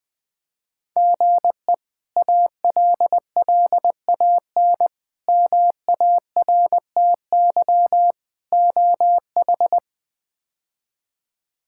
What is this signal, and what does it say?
Morse code, heard as a keyed tone, about 20 wpm, 700 Hz. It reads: GE ALLAN MARTY OH
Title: K1USN SST QSO Practice - 20wpm